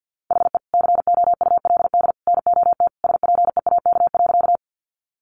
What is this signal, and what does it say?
Morse code, heard as a keyed tone, about 50 wpm, 700 Hz. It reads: HE COULD NOT SPEAK.
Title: Sentences from Top 1100 Words 50wpm